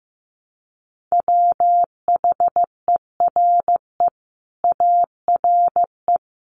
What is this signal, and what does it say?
Morse code, heard as a keyed tone, about 15 wpm, 700 Hz. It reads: WHERE ARE
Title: Sentences from Top 200 Words 15wpm